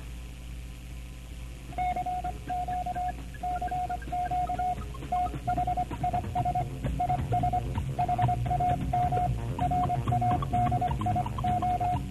Morse code, heard as a keyed tone, about 24 wpm, 720 Hz, with noise. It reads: CQ CQ THIS IS HAM RADIO